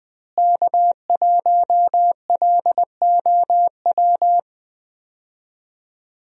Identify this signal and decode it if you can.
Morse code, heard as a keyed tone, about 20 wpm, 700 Hz. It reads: K1LOW
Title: USA Call Signs - 5 Characters - Repeated 3x - 20wpm